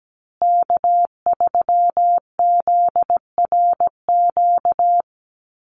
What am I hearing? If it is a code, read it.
Morse code, heard as a keyed tone, about 17 wpm, 700 Hz. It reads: K3ZRQ
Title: USA Call Signs Repeated 3x 17wpm